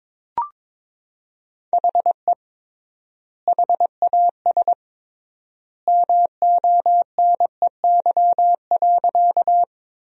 Morse code, heard as a keyed tone, about 22 wpm, 700 Hz. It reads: HE HAS MONEY.
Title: 3x Word-Spacing - Sentences from Top 400 Words - 22wpm